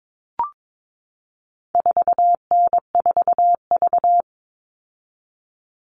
Morse code, heard as a keyed tone, about 22 wpm, 700 Hz. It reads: 4N4V
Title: All Cut Numbers - 4-Digit 22wpm